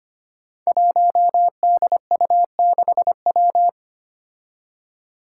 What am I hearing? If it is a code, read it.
Morse code, heard as a keyed tone, about 25 wpm, 700 Hz. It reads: 1DU6W
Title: Five Challenging Letter-Number Combinations 25wpm